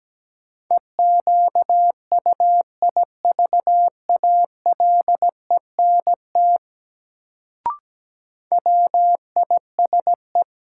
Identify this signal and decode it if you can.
Morse code, heard as a keyed tone, about 17 wpm, 700 Hz. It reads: EQUIVALENT WISE
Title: Sentences from Top 1600 Words - New Word Drill Alt - 17wpm